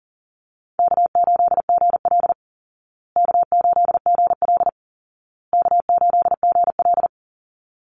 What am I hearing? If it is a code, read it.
Morse code, heard as a keyed tone, about 40 wpm, 700 Hz. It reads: K8GL K8GL K8GL
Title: Easy Call Signs Repeated 3x - 40wpm